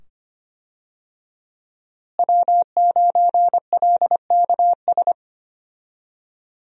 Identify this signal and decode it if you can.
Morse code, heard as a keyed tone, about 25 wpm, 700 Hz. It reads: W9LKS
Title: USA Call Signs - 5 Characters - Repeated 2x - 25wpm